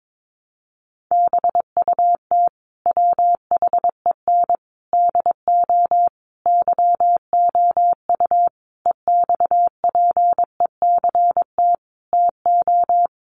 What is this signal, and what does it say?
Morse code, heard as a keyed tone, about 22 wpm, 700 Hz. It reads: BUT WHEN DO YOU EXPECT TO
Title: Sentences from Top 1100 Words 22wpm